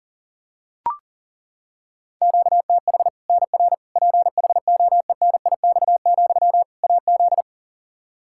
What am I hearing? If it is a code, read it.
Morse code, heard as a keyed tone, about 40 wpm, 700 Hz. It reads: QTH NR PHOENIX, AZ
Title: Practice QSO - Exchange 1 - Beginning - 40wpm